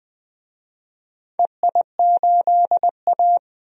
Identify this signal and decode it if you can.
Morse code, heard as a keyed tone, about 20 wpm, 700 Hz. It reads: EI8A
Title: DXCC-100 Call Signs - 4 Characters - 20wpm